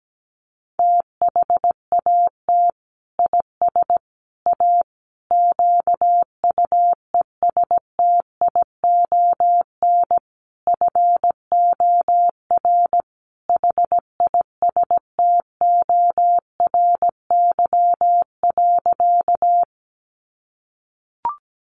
Morse code, heard as a keyed tone, about 17 wpm, 700 Hz. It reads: THAT IS A QUESTION FOR HISTORY.
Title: Sentences from Top 600 Words 17wpm